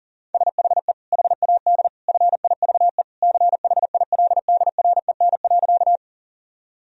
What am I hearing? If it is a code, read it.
Morse code, heard as a keyed tone, about 37 wpm, 700 Hz. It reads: SHE HAD FIVE CHILDREN.
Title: Sentences from Top 1000 Words - Review Part 2 - 40wpm